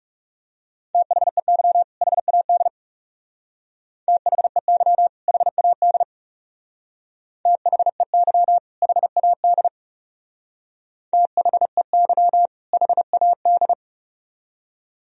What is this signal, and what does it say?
Morse code, heard as a keyed tone, about 36 wpm, 700 Hz. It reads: THEY HAD THEY HAD THEY HAD THEY HAD
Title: Speed-Racing - Sets of 2 Words 30wpm